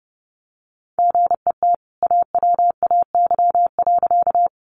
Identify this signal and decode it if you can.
Morse code, heard as a keyed tone, about 29 wpm, 700 Hz. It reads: GET AWAY.
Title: Sentences from Top 600 Words 30wpm